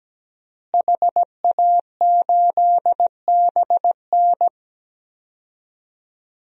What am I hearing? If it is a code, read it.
Morse code, heard as a keyed tone, about 17 wpm, 700 Hz. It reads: HA8BN
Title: DXCC-100 Call Signs - 5 Characters - Repeated 2x - 17wpm